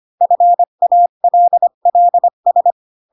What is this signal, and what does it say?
Morse code, heard as a keyed tone, about 25 wpm, 700 Hz. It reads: FALLS
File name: Pinocchio / Pinocchio ch6 20 25 ik1hge